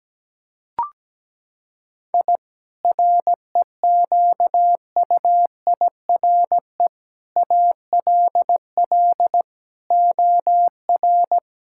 Morse code, heard as a keyed tone, about 17 wpm, 700 Hz. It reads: I REQUIRE ALL OR
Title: Sentences from Top 1000 Words - Review Part 6 - 17wpm